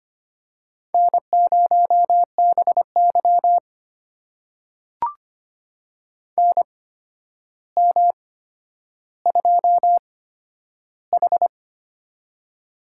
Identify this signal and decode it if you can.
Morse code, heard as a keyed tone, about 25 wpm, spaced as for 11 wpm, 700 Hz. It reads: N0BY NM2H
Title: USA Call Signs - 4 Characters - 8x Character Spacing - 25wpm